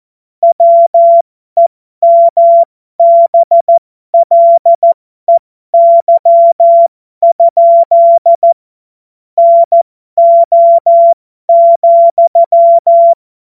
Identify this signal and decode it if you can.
Morse code, heard as a keyed tone, about 14 wpm, spaced as for 12 wpm, 680 Hz. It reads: WEMBLEY? NO,
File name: Morsekurs / Witze